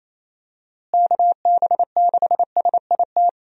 Morse code, heard as a keyed tone, about 28 wpm, 700 Hz. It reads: KB6SIT
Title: USA Call Signs - 6 Characters - Repeated 3x - 28wpm